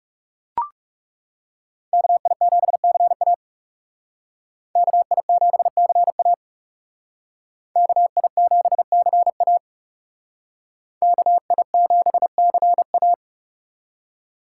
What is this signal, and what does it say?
Morse code, heard as a keyed tone, about 39 wpm, 700 Hz. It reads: KI7CA KI7CA KI7CA KI7CA